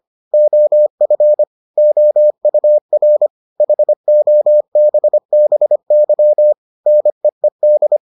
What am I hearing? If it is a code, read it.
Morse code, heard as a keyed tone, about 25 wpm, 600 Hz. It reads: OF OUR HOBBY NEED